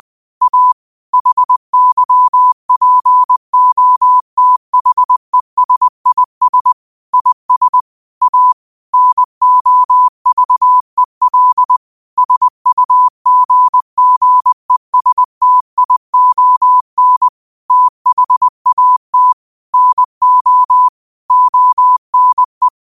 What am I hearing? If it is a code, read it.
Morse code, heard as a keyed tone, about 20 wpm, 1000 Hz. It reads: A HYPOTHESIS IS A NOVEL SUGGESTION THAT NO ONE